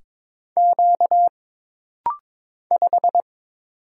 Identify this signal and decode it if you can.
Morse code, heard as a keyed tone, about 22 wpm, 700 Hz. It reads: Q 5
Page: This rapid-fire timing is geared to push your ICR (Instant Character Recognition) skills.